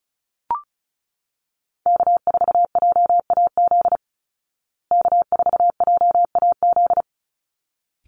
Morse code, heard as a keyed tone, about 34 wpm, 700 Hz. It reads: K4JAZ K4JAZ
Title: Call Signs Repeated 3x - 35wpm